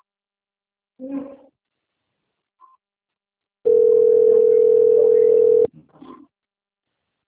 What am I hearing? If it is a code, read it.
Ringback tone (North America), 440 + 480 Hz.
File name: rg-1600-unknown- - - .7271